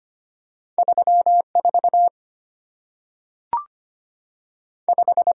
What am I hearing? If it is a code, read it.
Morse code, heard as a keyed tone, about 25 wpm, 700 Hz. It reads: 34 5
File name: Two Letter-Number Combinations - 25wpm